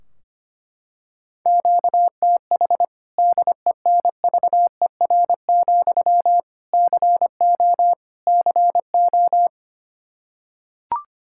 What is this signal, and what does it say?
Morse code, heard as a keyed tone, about 25 wpm, 700 Hz. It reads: QTH DENVER, CO CO
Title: K1USN SST QSO Practice 25wpm